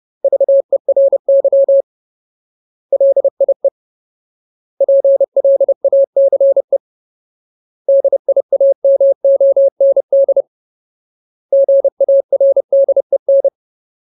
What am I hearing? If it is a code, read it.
Morse code, heard as a keyed tone, about 30 wpm, 550 Hz. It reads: VERY LIE PLACE DIAMOND GARDEN
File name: Ultimate Word List / 30WPM Ultimate